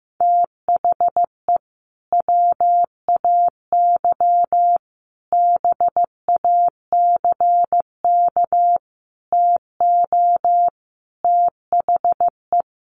Morse code, heard as a keyed tone, about 15 wpm, 700 Hz. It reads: THE WAY BACK TO THE
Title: Sets of 5 Words Encore2 15wpm